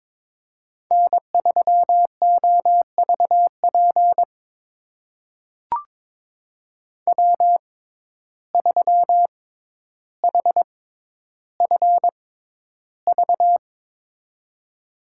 Morse code, heard as a keyed tone, about 22 wpm, spaced as for 11 wpm, 700 Hz. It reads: N3OVP W3HFV
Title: USA Call Signs - 6x Character Spacing - 22wpm